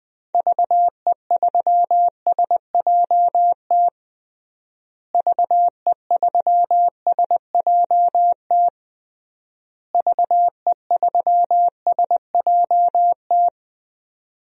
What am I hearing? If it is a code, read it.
Morse code, heard as a keyed tone, about 21 wpm, 700 Hz. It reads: VE3SJT VE3SJT VE3SJT
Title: DXCC-100 Call Signs - 6 Characters - Repeated 3x - 20wpm